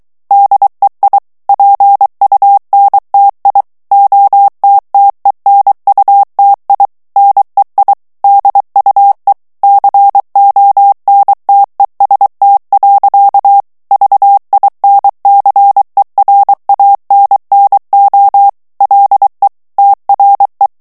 Morse code, heard as a keyed tone, about 23 wpm, 800 Hz. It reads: DEI PUNTI OTTENUTI NEI DUE CONTEST. VINCERANNO LE TRE
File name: ITALIEN texte clair 2 vitesse 22 mots minute